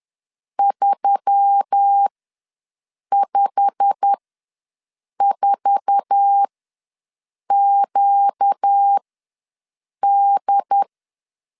Morse code, heard as a keyed tone, about 10 wpm, spaced as for 6 wpm, 800 Hz. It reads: 354QD
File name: LEZIONI / 8 YJ45 / 8 YJ45 ws 10wpm 3843